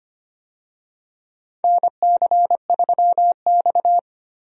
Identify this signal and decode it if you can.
Morse code, heard as a keyed tone, about 25 wpm, 700 Hz. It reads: NC3X
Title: USA Call Signs - 4 Characters - Repeated 2x - 25wpm